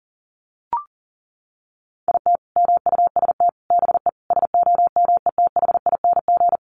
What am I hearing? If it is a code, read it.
Morse code, heard as a keyed tone, about 40 wpm, 700 Hz. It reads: IT MUST BE SOMETHING
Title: Sentences from Top 600 Words 40wpm